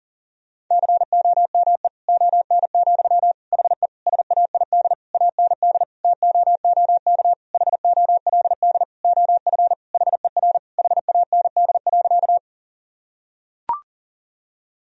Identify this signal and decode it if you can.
Morse code, heard as a keyed tone, about 40 wpm, 700 Hz. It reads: COME ON, HE SAID AND TOOK HOLD OF HER HAND.
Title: Sentences from Top 1000 Words - Review Part 2 - 40wpm